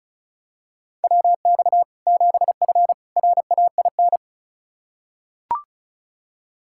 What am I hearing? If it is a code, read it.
Morse code, heard as a keyed tone, about 36 wpm, 700 Hz. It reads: WX 7F RAIN
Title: Practice QSO - Exchange 2 - Info - 35wpm